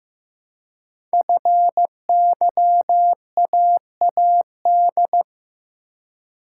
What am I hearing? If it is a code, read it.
Morse code, heard as a keyed tone, about 15 wpm, 700 Hz. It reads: FYAAD
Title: Code Groups - 5 Letters - 15wpm